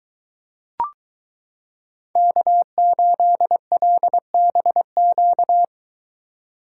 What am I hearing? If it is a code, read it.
Morse code, heard as a keyed tone, about 22 wpm, 700 Hz. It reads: K8LBQ